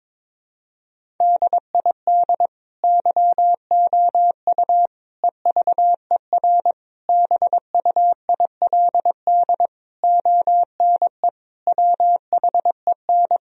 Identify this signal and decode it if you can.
Morse code, heard as a keyed tone, about 22 wpm, 700 Hz. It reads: DID YOU EVER BUILD ONE WHEN
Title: Sentences from Top 1000 Words - Review Part 1 - 22wpm